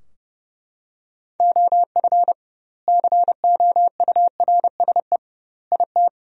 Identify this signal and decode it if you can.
Morse code, heard as a keyed tone, about 30 wpm, 700 Hz. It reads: OF COURSE IT
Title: 2x Word-Spacing - Sentences from Top 500 Words - 30wpm